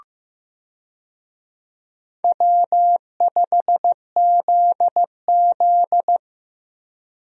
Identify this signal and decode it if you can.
Morse code, heard as a keyed tone, about 16 wpm, 700 Hz. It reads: W5ZZ
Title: USA Call Signs - 4 Characters - Repeated 3x - 15wpm